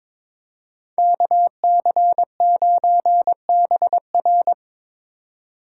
Morse code, heard as a keyed tone, about 25 wpm, 700 Hz. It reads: KC9BR
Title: Speed Racing - USA 2x2 Call Signs 22wpm